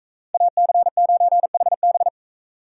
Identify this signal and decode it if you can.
Morse code, heard as a keyed tone, about 44 wpm, 700 Hz. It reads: AC9HB